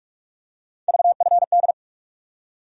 Morse code, heard as a keyed tone, about 47 wpm, 700 Hz. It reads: VFD
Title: USA Call Sign Postfixes - 3 Letters 45wpm